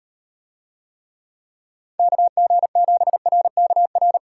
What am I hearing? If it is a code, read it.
Morse code, heard as a keyed tone, about 38 wpm, 700 Hz. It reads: KG7RKR